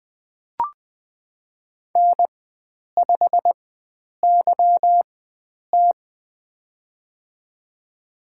Morse code, heard as a keyed tone, about 20 wpm, spaced as for 9 wpm, 700 Hz. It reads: N5YT
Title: USA Call Signs - 4 Characters - 4x Character Spacing - 20wpm